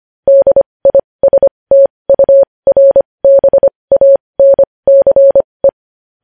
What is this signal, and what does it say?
Morse code, heard as a keyed tone, about 25 wpm, spaced as for 20 wpm, 550 Hz. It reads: DISTURBANCE
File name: podcast / News headlines-1x-25x20